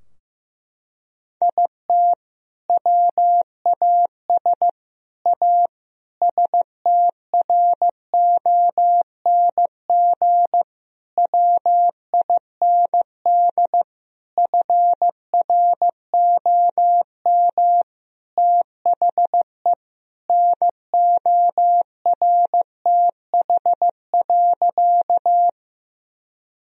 Morse code, heard as a keyed tone, about 15 wpm, 700 Hz. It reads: IT WAS A STRONG WIND FROM THE NORTH.
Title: Sentences from Top 1000 Words - Review Part 2 - 15wpm